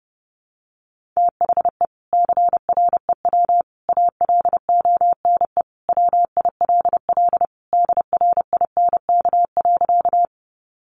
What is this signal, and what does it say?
Morse code, heard as a keyed tone, about 30 wpm, 700 Hz. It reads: THE CREW ALONE WILL DRINK.
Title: Sentences from Top 1500 Words 30wpm